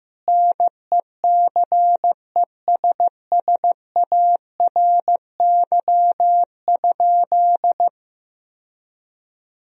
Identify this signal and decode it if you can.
Morse code, heard as a keyed tone, about 15 wpm, 700 Hz. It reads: NECESSARY?
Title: Sentences from Top 1000 Words - Review Part 3 - 15wpm